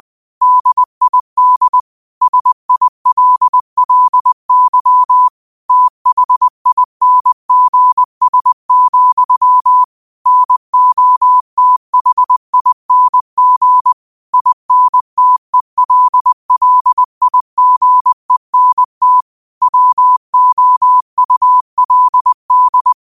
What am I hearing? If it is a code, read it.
Morse code, heard as a keyed tone, about 20 wpm, 1000 Hz. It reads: DID SILLY THINGS, NOTHING INTELLIGENT WOULD
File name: qotdfiles / 20wpm